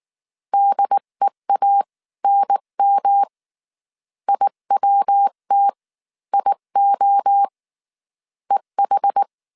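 Morse code, heard as a keyed tone, about 19 wpm, spaced as for 17 wpm, 800 Hz. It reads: DEANM IWT IO EH
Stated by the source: Morse code dictation at 18 wpm from the IW3SGT QRS Morse code training course, five-character groups of 300 characters.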